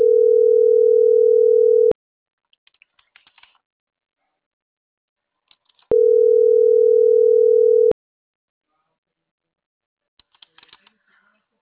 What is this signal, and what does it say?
Ringback tone (North America), 440 + 480 Hz.